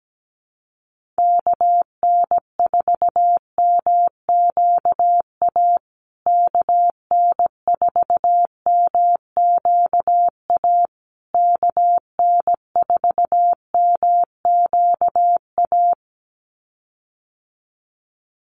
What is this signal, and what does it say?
Morse code, heard as a keyed tone, about 17 wpm, 700 Hz. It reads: KN4MQA KN4MQA KN4MQA
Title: USA Call Signs Repeated 3x 17wpm